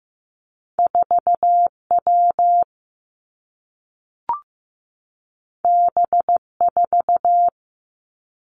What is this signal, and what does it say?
Morse code, heard as a keyed tone, about 15 wpm, 700 Hz. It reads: 4 W B 4
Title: Two Challenging Letter-Number Combinations 15wpm